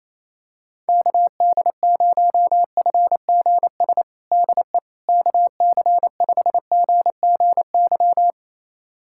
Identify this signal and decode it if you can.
Morse code, heard as a keyed tone, about 28 wpm, 700 Hz. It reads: KD0FGS DE KC5GGY
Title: USA Callsign DE Callsign - 6 Characters - 28wpm